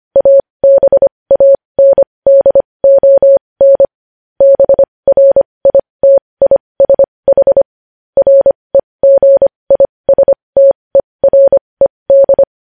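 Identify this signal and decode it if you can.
Morse code, heard as a keyed tone, about 25 wpm, spaced as for 20 wpm, 550 Hz. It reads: ABANDON BRITISH REGISTERED
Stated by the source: Morse practice sending news headlines at 20 wpm.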